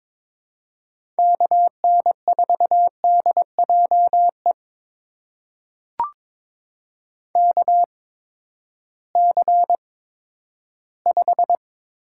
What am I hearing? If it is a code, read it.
Morse code, heard as a keyed tone, about 22 wpm, spaced as for 12 wpm, 700 Hz. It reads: KN4DJE KC5
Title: USA Call Signs - 6 Characters - 8x Character Spacing - 22wpm